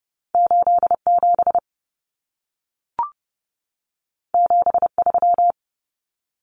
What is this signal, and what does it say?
Morse code, heard as a keyed tone, about 30 wpm, 700 Hz. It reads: 87 73
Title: Common Cut Numbers - 2-Digit 30wpm